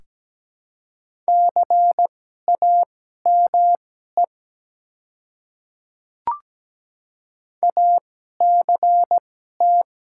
Morse code, heard as a keyed tone, about 17 wpm, spaced as for 12 wpm, 700 Hz. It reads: CAME ACT
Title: 2x Character Spacing - Top 200 Words - 17wpm